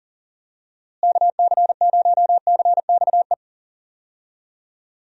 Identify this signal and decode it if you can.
Morse code, heard as a keyed tone, about 40 wpm, 700 Hz. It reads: KC0CXE
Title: USA Call Signs - 6 Characters - Repeated 2x - 40wpm